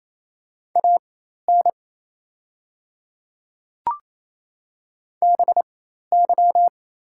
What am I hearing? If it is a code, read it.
Morse code, heard as a keyed tone, about 27 wpm, spaced as for 13 wpm, 690 Hz. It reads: AN BY
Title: 4x Character Spacing - Top 300 Words - 28wpm